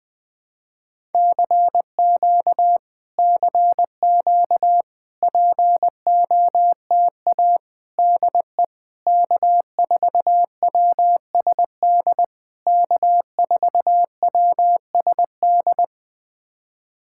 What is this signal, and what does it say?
Morse code, heard as a keyed tone, about 20 wpm, 700 Hz. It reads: CQ CQ POTA DE K4WSD K4WSD
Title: POTA QSO Practice - 20wpm